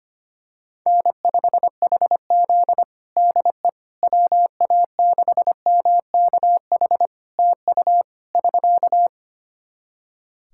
Morse code, heard as a keyed tone, about 25 wpm, 700 Hz. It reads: N5HZ DE WA6MKH TU <SK>
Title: Practice QSO - Exchange 4 - Ending - 25wpm